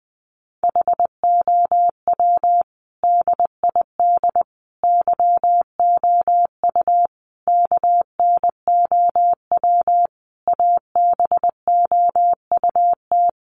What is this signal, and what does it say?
Morse code, heard as a keyed tone, about 20 wpm, 700 Hz. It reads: HOW DID YOU KNOW ABOUT
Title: Sentences from Top 100 Words 20wpm